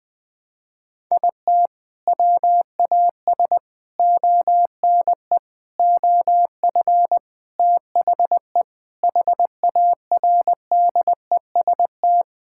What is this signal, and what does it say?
Morse code, heard as a keyed tone, about 21 wpm, 700 Hz. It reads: IT WAS ONE OF THE HARDEST
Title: Sentences with 5-Word Phrases - 20wpm